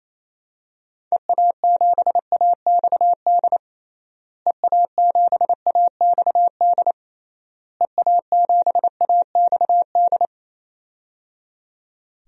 Morse code, heard as a keyed tone, about 28 wpm, 700 Hz. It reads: EA7AXD EA7AXD EA7AXD
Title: DXCC-100 Call Signs - 6 Characters - Repeated 3x - 28wpm